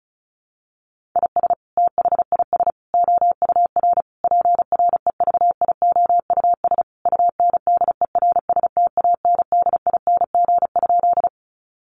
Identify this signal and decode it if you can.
Morse code, heard as a keyed tone, about 34 wpm, 700 Hz. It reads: IS THIS OUR PREVIOUS UNDERSTANDING?
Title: Sentences from Top 1500 Words 35wpm